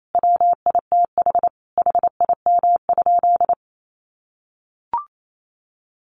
Morse code, heard as a keyed tone, about 28 wpm, 700 Hz. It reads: WITH HIM?
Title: Sentences from Top 700 Words 28wpm